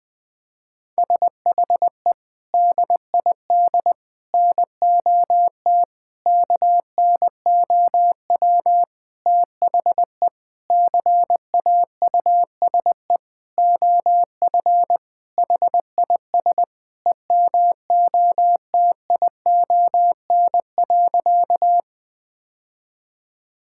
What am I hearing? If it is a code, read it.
Morse code, heard as a keyed tone, about 21 wpm, 700 Hz. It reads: SHE DID NOT KNOW THE CAUSE OF HIS EMOTION.
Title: Sentences from Top 1600 Words 20wpm